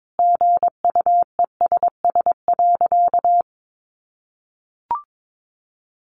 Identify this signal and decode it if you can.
Morse code, heard as a keyed tone, about 22 wpm, 700 Hz. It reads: GUESS.
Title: Sentences from Top 1500 Words 22wpm